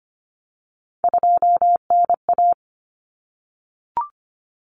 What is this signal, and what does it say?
Morse code, heard as a keyed tone, about 25 wpm, 700 Hz. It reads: 2NA
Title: Common Cut Numbers - 3-Digit 25wpm